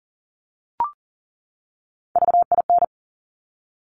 Morse code, heard as a keyed tone, about 40 wpm, 700 Hz. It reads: VIN
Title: Top Three Letter Combinations in Words 40wpm